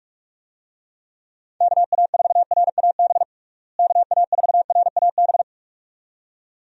Morse code, heard as a keyed tone, about 42 wpm, 700 Hz. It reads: KA4RAB KA4RAB
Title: USA Call Signs - 6 Characters - Repeated 2x - 45wpm